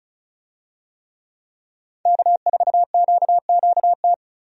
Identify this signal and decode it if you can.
Morse code, heard as a keyed tone, about 34 wpm, 700 Hz. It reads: K4QQT
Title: USA Call Signs - 5 Characters - 35wpm